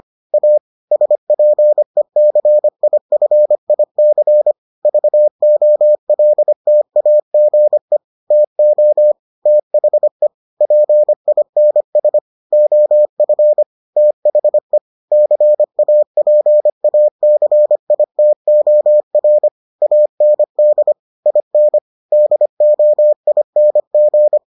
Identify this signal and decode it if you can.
Morse code, heard as a keyed tone, about 25 wpm, 600 Hz. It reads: A SPECIFIC VOLTAGE TO THE PINS OF THE CAPACITOR AND IN DOING